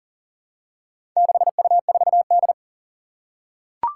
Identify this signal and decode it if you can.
Morse code, heard as a keyed tone, about 40 wpm, 700 Hz. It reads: 6U4D
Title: Four Challenging Letter-Number Combinations 40wpm